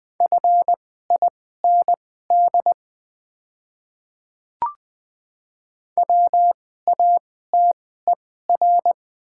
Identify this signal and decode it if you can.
Morse code, heard as a keyed tone, about 20 wpm, spaced as for 15 wpm, 700 Hz. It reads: FIND WATER
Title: 2x Character Spacing - Top 100 Words - 20wpm